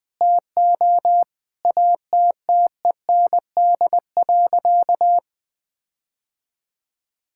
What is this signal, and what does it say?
Morse code, heard as a keyed tone, about 20 wpm, 700 Hz. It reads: TO ATTEND.
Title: Sentences with 7-Word Phrases - 20wpm